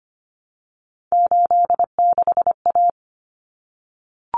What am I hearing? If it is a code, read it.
Morse code, heard as a keyed tone, about 25 wpm, 700 Hz. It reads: 86A
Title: Common Cut Numbers - 3-Digit 25wpm